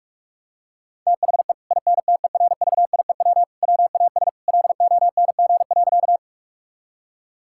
Morse code, heard as a keyed tone, about 46 wpm, 700 Hz. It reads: THE INTERVIEW WAS LONG.
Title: Sentences from Top 1600 Words 45wpm